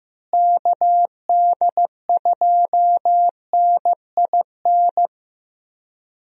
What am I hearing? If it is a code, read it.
Morse code, heard as a keyed tone, about 15 wpm, 700 Hz. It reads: KD2NIN
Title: USA Call Signs - 6 Characters - Repeated 3x - 15wpm